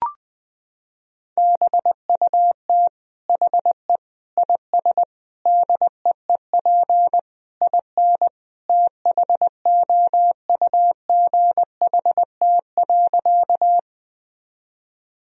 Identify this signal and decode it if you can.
Morse code, heard as a keyed tone, about 20 wpm, 700 Hz. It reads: BUT HE IS DEEP IN THOUGHT.
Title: Sentences from Top 1000 Words - Review Part 3 - 20wpm